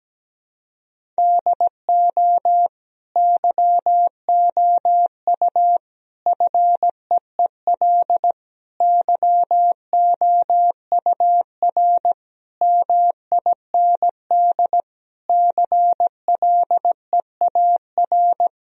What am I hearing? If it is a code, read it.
Morse code, heard as a keyed tone, about 17 wpm, 700 Hz. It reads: DO YOU FEEL YOUR MIND CLEAR
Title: Sentences from Top 1000 Words - Review Part 1 - 17wpm